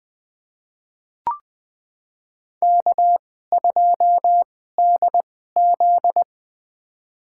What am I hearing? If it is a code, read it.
Morse code, heard as a keyed tone, about 20 wpm, spaced as for 14 wpm, 700 Hz. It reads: K2DZ
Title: USA Call Signs - 4 Characters - 2x Character Spacing - 20wpm